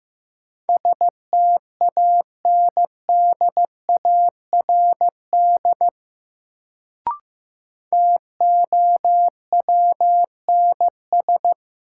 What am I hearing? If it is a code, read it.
Morse code, heard as a keyed tone, about 15 wpm, 700 Hz. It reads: STANDARD TOWNS
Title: Sentences from Top 1000 Words - Review Part 6 - New Word Drill - 15wpm